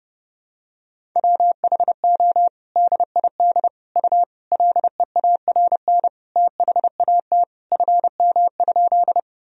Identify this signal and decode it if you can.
Morse code, heard as a keyed tone, about 32 wpm, 700 Hz. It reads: WHO DID U LEARN THAT FM?
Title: Sentences from Top 300 Words and Abbreviations 30wpm